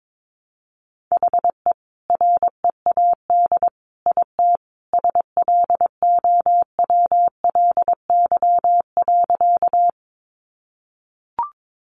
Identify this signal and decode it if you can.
Morse code, heard as a keyed tone, about 22 wpm, 700 Hz. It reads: HE READ IT SLOWLY.
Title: Sentences from Top 1100 Words 22wpm